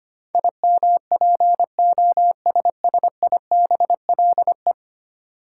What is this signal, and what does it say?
Morse code, heard as a keyed tone, about 25 wpm, 700 Hz. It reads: IMPOSSIBLE
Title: Sentences from Top 1100 Words - New Word Drill - 25wpm